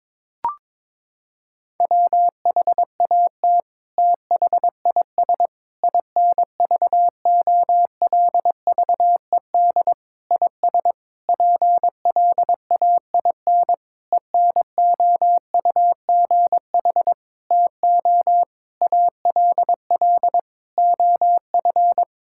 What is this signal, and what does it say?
Morse code, heard as a keyed tone, about 22 wpm, 700 Hz. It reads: WHAT THIS INVOLVED IS PLAIN ENOUGH TO ALL OF
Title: Sentences from Top 1000 Words - Review Part 5 - 22wpm